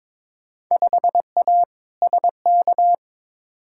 Morse code, heard as a keyed tone, about 22 wpm, 700 Hz. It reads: 5A SK
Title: Field Day Exchanges Encore - 22wpm